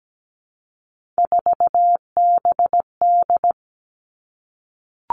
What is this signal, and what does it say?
Morse code, heard as a keyed tone, about 17 wpm, 700 Hz. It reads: 4BD
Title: Three Challenging Letter-Number Combinations 17wpm